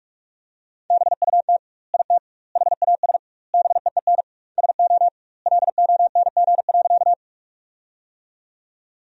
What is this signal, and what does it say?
Morse code, heard as a keyed tone, about 43 wpm, 700 Hz. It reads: BUT IT HAS BEEN SO LONG.
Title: 2x Word-Spacing - Sentences from Top 100 Words - 45wpm